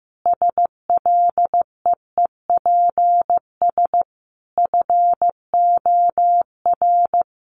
Morse code, heard as a keyed tone, about 15 wpm, 700 Hz. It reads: SLEEPS FOR
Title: Additional Easy Phrases 15wpm